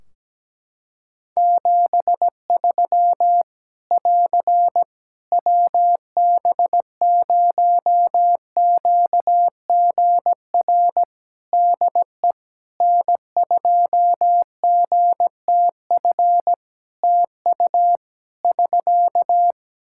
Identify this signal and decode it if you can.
Morse code, heard as a keyed tone, about 17 wpm, 700 Hz. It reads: 73 <AR> WB0QGR DE N2GTF TU <SK>
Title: Practice QSO - Exchange 4 - Ending - 17wpm